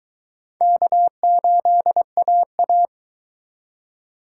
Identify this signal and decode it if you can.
Morse code, heard as a keyed tone, about 22 wpm, 700 Hz. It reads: K8AA